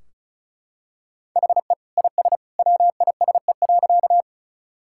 Morse code, heard as a keyed tone, about 36 wpm, 700 Hz. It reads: HE IS WISE.
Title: Sentences from Top 1600 Words 35wpm